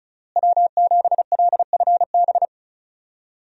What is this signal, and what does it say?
Morse code, heard as a keyed tone, about 36 wpm, 700 Hz. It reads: W7LFB
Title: USA Call Signs - 5 Characters - Repeated 3x - 35wpm